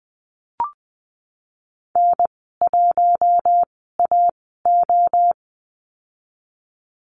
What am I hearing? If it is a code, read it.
Morse code, heard as a keyed tone, about 20 wpm, spaced as for 14 wpm, 700 Hz. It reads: N1AO
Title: 088 Lesson 15g - Number 5 - Call Signs 20wpm